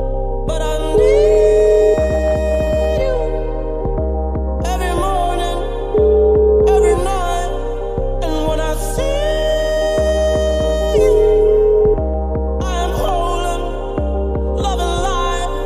Ringback tone (Europe, 425 Hz), with noise.